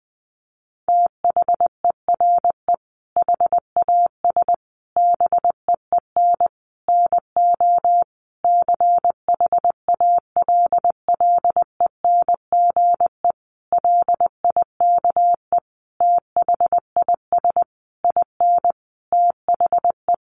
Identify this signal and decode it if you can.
Morse code, heard as a keyed tone, about 20 wpm, 700 Hz. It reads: THERE HAS BEEN NO CHALLENGE LIKE THIS IN THE
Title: Sentences from Top 1500 Words 20wpm